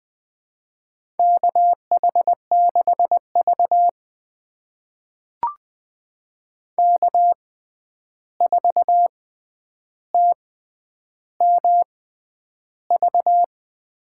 Morse code, heard as a keyed tone, about 20 wpm, spaced as for 10 wpm, 700 Hz. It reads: KH6V K4TMV
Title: USA Call Signs - 6x Character Spacing - 20wpm